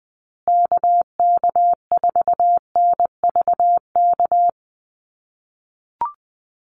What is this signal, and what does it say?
Morse code, heard as a keyed tone, about 20 wpm, 700 Hz. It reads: KK4NVK
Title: USA Call Signs Repeated 2x 20wpm